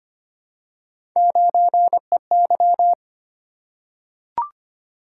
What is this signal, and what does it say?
Morse code, heard as a keyed tone, about 25 wpm, 700 Hz. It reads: 9EY
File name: Three Letter-Number Combinations - 25wpm